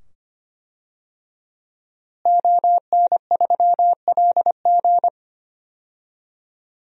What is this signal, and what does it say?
Morse code, heard as a keyed tone, about 25 wpm, 700 Hz. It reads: ON3LG
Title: DXCC-100 Call Signs - 5 Characters - Repeated 2x - 25wpm